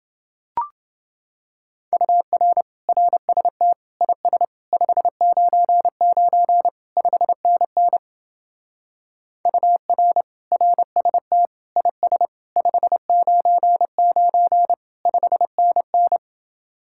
Morse code, heard as a keyed tone, about 27 wpm, 700 Hz. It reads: UR RST IS 599 5NN UR RST IS 599 5NN